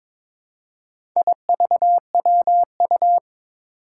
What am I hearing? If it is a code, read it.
Morse code, heard as a keyed tone, about 23 wpm, 700 Hz. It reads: IVWU
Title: Four Challenging Letter-Number Combinations 22wpm